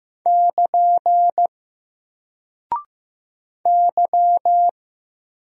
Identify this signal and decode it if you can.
Morse code, heard as a keyed tone, about 15 wpm, 700 Hz. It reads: <KN> Y
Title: Single Character (Letter:Number:Punctuation:Prosign) 15wpm